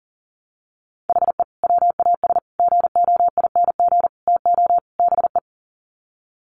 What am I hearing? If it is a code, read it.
Morse code, heard as a keyed tone, about 40 wpm, 700 Hz. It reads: HE WAS GOING TO BE
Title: Sets of 5 Words 40wpm